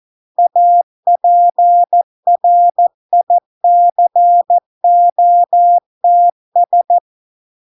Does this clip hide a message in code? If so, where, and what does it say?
Yes: Morse code from 0.4 to 7.0 s, heard as a keyed tone, about 14 wpm, 700 Hz. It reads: APRICOTS